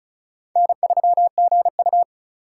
Morse code, heard as a keyed tone, about 36 wpm, 700 Hz. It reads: N3GU
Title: USA Call Signs - 4 Characters - 6x Character Spacing - 35wpm